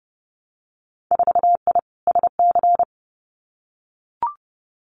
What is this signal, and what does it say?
Morse code, heard as a keyed tone, about 30 wpm, 700 Hz. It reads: 4I SC
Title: Winter Field Day Exchanges 30wpm